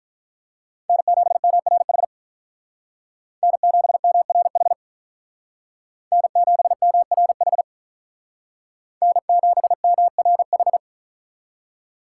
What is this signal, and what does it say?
Morse code, heard as a keyed tone, about 43 wpm, 700 Hz. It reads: N7MRH N7MRH N7MRH N7MRH
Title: Speed Racing - USA 1x3 Call Signs 35wpm